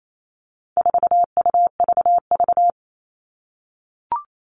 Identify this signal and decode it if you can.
Morse code, heard as a keyed tone, about 28 wpm, 700 Hz. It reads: 4UVV
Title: All Cut Numbers - 4-Digit 28wpm